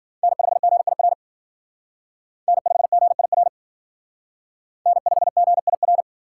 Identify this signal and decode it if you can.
Morse code, heard as a keyed tone, about 52 wpm, 700 Hz. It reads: N5GIR N5GIR N5GIR